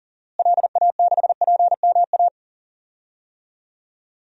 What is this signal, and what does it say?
Morse code, heard as a keyed tone, about 40 wpm, 700 Hz. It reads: LA6PMA
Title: DXCC-100 Call Signs - 6 Characters - Repeated 2x - 40wpm